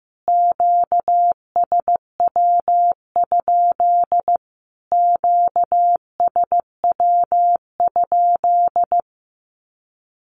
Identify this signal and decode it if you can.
Morse code, heard as a keyed tone, about 14 wpm, 700 Hz. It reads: QSW? QSW?
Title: Q-Codes Repeated 3x 15wpm